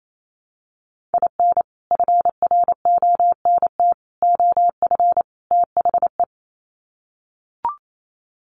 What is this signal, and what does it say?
Morse code, heard as a keyed tone, about 28 wpm, 700 Hz. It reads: IN FRONT OF THE
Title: Sets of 4 Words 28wpm